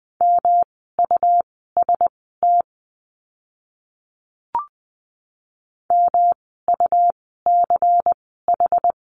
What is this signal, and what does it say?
Morse code, heard as a keyed tone, about 20 wpm, spaced as for 15 wpm, 700 Hz. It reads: MUST MUCH
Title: 237 Lesson 38f - Letter q - Words 20wpm